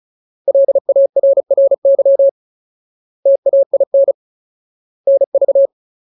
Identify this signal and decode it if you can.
Morse code, heard as a keyed tone, about 34 wpm, 550 Hz. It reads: LARRY TAIN NV
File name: Ultimate Word List / 35WPM Ultimate